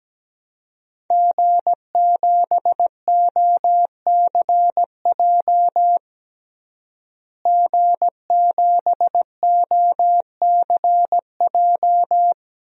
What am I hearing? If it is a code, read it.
Morse code, heard as a keyed tone, about 17 wpm, 700 Hz. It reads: G7OCJ G7OCJ
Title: DXCC-100 Call Signs - 5 Characters - Repeated 3x - 17wpm